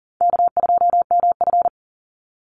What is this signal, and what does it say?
Morse code, heard as a keyed tone, about 40 wpm, 700 Hz. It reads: K2MF
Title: Easy Call Signs Repeated 2x - 40wpm